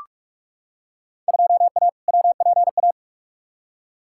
Morse code, heard as a keyed tone, about 46 wpm, 700 Hz. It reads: 2A WPA
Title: Field Day Exchanges - 45wpm